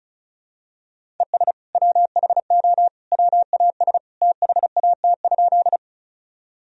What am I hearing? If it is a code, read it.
Morse code, heard as a keyed tone, about 34 wpm, 700 Hz. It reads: ES WHO WAS THAT?
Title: Sentences from Top 100 Words and Abbreviations 35wpm